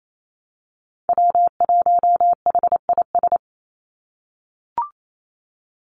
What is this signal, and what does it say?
Morse code, heard as a keyed tone, about 28 wpm, 700 Hz. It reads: W1HIS
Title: Call Signs - 28wpm